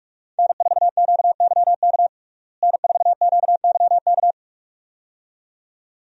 Morse code, heard as a keyed tone, about 46 wpm, 700 Hz. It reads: N4QYK N4QYK
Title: USA Call Signs - 5 Characters - Repeated 2x - 45wpm